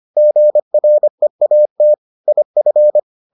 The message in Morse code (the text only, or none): GREAT IF